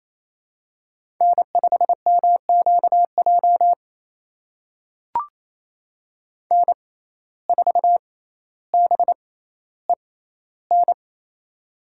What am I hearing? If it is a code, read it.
Morse code, heard as a keyed tone, about 27 wpm, spaced as for 14 wpm, 700 Hz. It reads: N5MQJ N4BEN
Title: USA Call Signs - 5 Characters - 6x Character Spacing - 28wpm